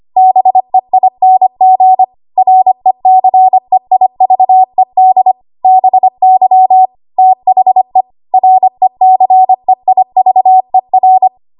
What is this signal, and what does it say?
Morse code, heard as a keyed tone, about 25 wpm, 750 Hz. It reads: BEING RECEIVED BY THE RECEIVER
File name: Archive / 25 WPM / 25WPM